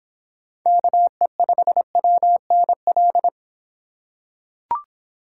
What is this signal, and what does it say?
Morse code, heard as a keyed tone, about 25 wpm, 700 Hz. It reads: KE5WNL